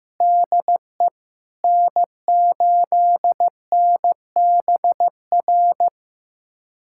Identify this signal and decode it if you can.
Morse code, heard as a keyed tone, about 15 wpm, 700 Hz. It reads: DE N8NBR
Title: USA Callsign DE Callsign - 5 Characters - 15wpm